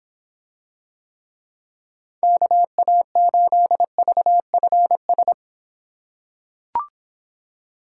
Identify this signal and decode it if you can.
Morse code, heard as a keyed tone, about 24 wpm, 700 Hz. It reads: KA8VFS